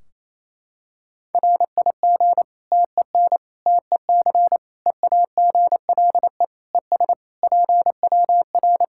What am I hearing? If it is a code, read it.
Morse code, heard as a keyed tone, about 28 wpm, 700 Hz. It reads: RIG TEN TEC EAGLE ES PWR
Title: Practice QSO - Exchange 2 - Info - 28wpm